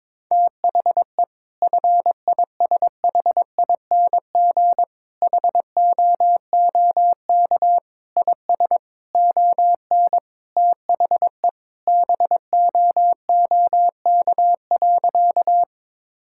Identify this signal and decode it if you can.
Morse code, heard as a keyed tone, about 22 wpm, 700 Hz. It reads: THE FISHING HOOK IS ON THE BOOK.
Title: Sentences from Top 1600 Words 22wpm